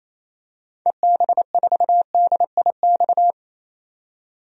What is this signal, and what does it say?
Morse code, heard as a keyed tone, about 28 wpm, 700 Hz. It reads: EB4DIX
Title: DXCC-100 Call Signs - 6 Characters - 28wpm